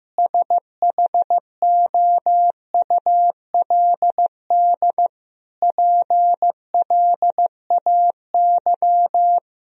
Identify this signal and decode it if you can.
Morse code, heard as a keyed tone, about 15 wpm, 700 Hz. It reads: SHOULD PLAY
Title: Sentences from Top 1000 Words - Review Part 3 - 15wpm